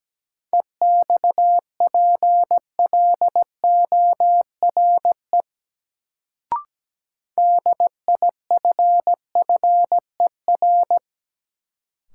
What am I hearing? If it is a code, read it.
Morse code, heard as a keyed tone, about 17 wpm, 700 Hz. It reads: EXPLORE DIFFER
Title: Sentences from Top 1100 Words - New Word Drill Alt - 17wpm